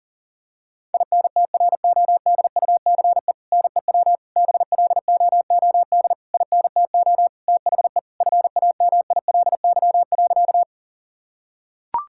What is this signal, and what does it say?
Morse code, heard as a keyed tone, about 40 wpm, 700 Hz. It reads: INTRODUCE NEW BLOOD INTO THE FAMILY.
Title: Sentences from Top 1600 Words 40wpm